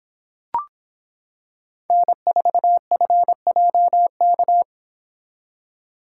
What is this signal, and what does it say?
Morse code, heard as a keyed tone, about 26 wpm, 700 Hz. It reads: N4FJK